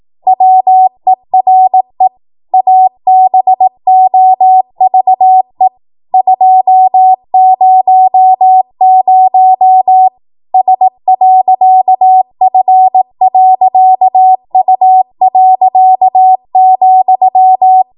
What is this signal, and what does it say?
Morse code, heard as a keyed tone, about 18 wpm, 750 Hz. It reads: WERE ABOVE 200 S.F.U.,
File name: Morse / 18WPM